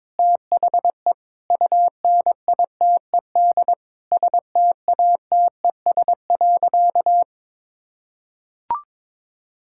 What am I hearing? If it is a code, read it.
Morse code, heard as a keyed tone, about 22 wpm, 700 Hz. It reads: THE UNITED STATES.
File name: practice / 22wpm / Sentences with 4-Word Phrases - 22wpm